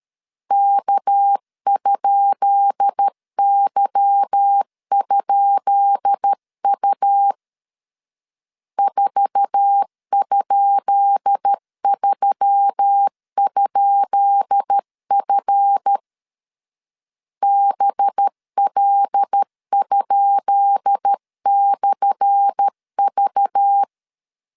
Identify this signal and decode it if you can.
Morse code, heard as a keyed tone, about 13 wpm, 800 Hz. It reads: K?Y?U 4?3?F BL?/V
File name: LEZIONI / 10 SIGNS / 10B INT 12wpm 2414